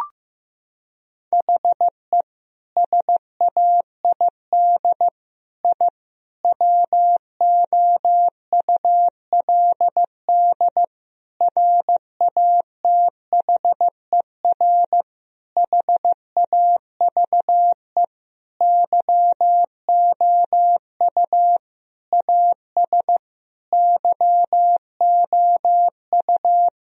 Morse code, heard as a keyed tone, about 16 wpm, 700 Hz. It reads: HE SAID I WOULD RATHER HAVE YOU AS YOU
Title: Sentences from Top 1000 Words - Review Part 4 - 15wpm